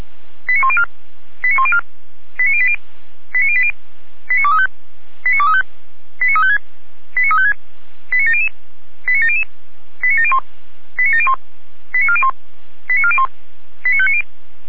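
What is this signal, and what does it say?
Five-tone selective call (ZVEI), heard as quick runs of five tones, tones of about 70 ms. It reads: 89104 89104 89080 89080 89235 89235 89356 89356 89700 89700 89701 89701 89401 89401 89590
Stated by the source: Five-tone ZVEI selective call 89020, received on SDR.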